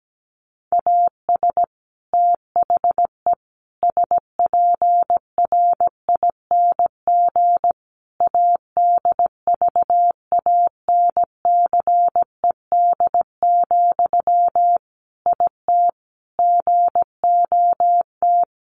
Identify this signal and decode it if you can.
Morse code, heard as a keyed tone, about 17 wpm, 700 Hz. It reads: AS THE SPRING ADVANCED, IT GOT
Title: Sentences from Top 1200 Words 17wpm